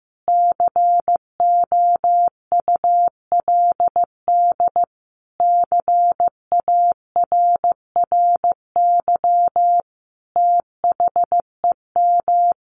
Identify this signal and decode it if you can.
Morse code, heard as a keyed tone, about 15 wpm, 700 Hz. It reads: COULD CARRY THEM
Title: Sentences from Top 1200 Words 15wpm